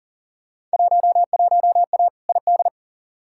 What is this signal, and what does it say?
Morse code, heard as a keyed tone, about 40 wpm, 700 Hz. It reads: 11A ID
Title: Field Day Exchanges Encore - 40wpm